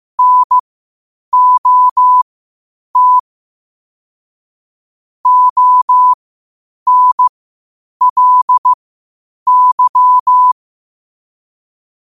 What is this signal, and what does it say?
Morse code, heard as a keyed tone, about 15 wpm, spaced as for 8 wpm, 1000 Hz. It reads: NOT ONLY
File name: qotdfiles / 8wpm